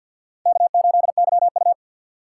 Morse code, heard as a keyed tone, about 50 wpm, 700 Hz. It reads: K8YU
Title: USA Call Signs - 4 Characters - 2x Character Spacing - 50wpm